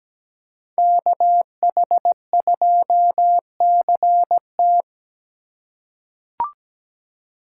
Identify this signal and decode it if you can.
Morse code, heard as a keyed tone, about 17 wpm, 700 Hz. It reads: KH2CT
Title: USA Call Signs - 4x Character Spacing - 17wpm